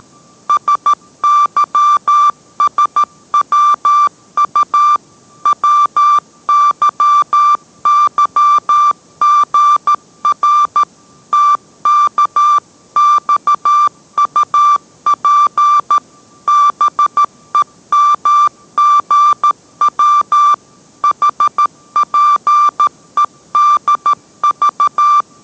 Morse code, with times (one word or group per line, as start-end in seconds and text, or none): 0.5-5.0 SYSWU
5.5-10.8 WYYGR
11.3-16.0 TKXUP
16.5-20.5 BEMGW
21.0-25.2 HPEDV